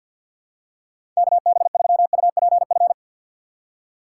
Morse code, heard as a keyed tone, about 52 wpm, 700 Hz. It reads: KB3UPF
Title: USA Call Signs - 6 Characters - 3x Character Spacing - 50wpm